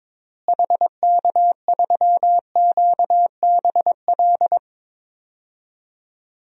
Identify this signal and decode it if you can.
Morse code, heard as a keyed tone, about 22 wpm, 700 Hz. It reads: HK3QBL
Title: DXCC-100 Call Signs - 6 Characters - Repeated 2x - 22wpm